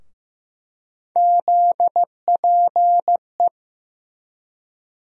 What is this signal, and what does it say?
Morse code, heard as a keyed tone, about 15 wpm, 700 Hz. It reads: ZPE
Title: USA Call Sign Postfixes - 3 Letters 15wpm